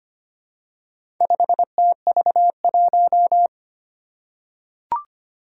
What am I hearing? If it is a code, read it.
Morse code, heard as a keyed tone, about 25 wpm, 700 Hz. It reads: 5TV1
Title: Four Challenging Letter-Number Combinations 25wpm